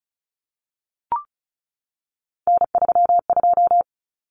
Morse code, heard as a keyed tone, about 34 wpm, 700 Hz. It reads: N32
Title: Common Cut Numbers - 3-Digit 35wpm